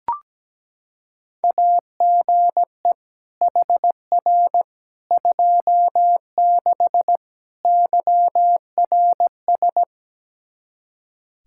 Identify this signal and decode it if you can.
Morse code, heard as a keyed tone, about 17 wpm, 700 Hz. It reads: AGE HR 26 YRS
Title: Practice QSO - Exchange 3 - More Info - 17wpm